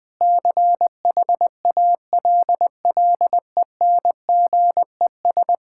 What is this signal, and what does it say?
Morse code, heard as a keyed tone, about 20 wpm, 700 Hz. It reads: CHALLENGES
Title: Sentences with 5-Word Phrases - 20wpm